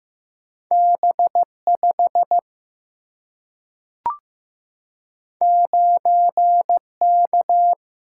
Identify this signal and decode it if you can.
Morse code, heard as a keyed tone, about 15 wpm, 700 Hz. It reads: B5 9K
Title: Single-Letter Single-Number 15wpm